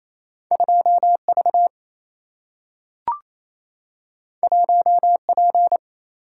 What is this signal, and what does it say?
Morse code, heard as a keyed tone, about 28 wpm, 700 Hz. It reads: 2 V 1 P
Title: Single-Letter Single-Number 28wpm